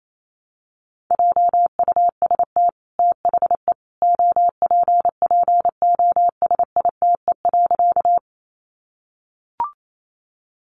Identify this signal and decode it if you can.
Morse code, heard as a keyed tone, about 28 wpm, 700 Hz. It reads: JUST THE OPPOSITE.
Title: Sentences from Top 1300 Words 28wpm